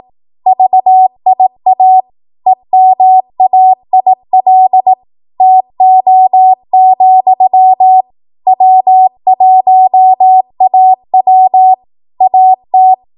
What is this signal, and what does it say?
Morse code, heard as a keyed tone, about 18 wpm, 750 Hz. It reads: VIA EMAIL TO, W1AW AT
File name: Archive / 18 WPM / 18WPM